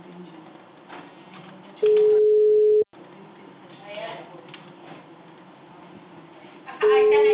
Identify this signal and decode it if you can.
Ringback tone (Europe, 425 Hz), with speech.